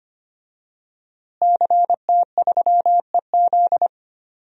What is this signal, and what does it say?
Morse code, heard as a keyed tone, about 25 wpm, 700 Hz. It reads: CT3EZ
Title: DXCC-100 Call Signs - 5 Characters - Repeated 3x - 25wpm